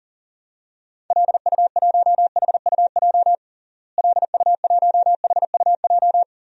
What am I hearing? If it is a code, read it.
Morse code, heard as a keyed tone, about 37 wpm, 700 Hz. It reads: LU1HUJ LU1HUJ
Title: DXCC-100 Call Signs - 6 Characters - Repeated 3x - 40wpm